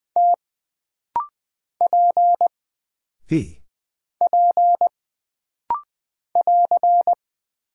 Morse code, heard as a keyed tone, about 20 wpm, 700 Hz. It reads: T P P <AR>
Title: Single Character (Letter:Number:Punctuation:Prosign) - Rapid-Fire - 20wpm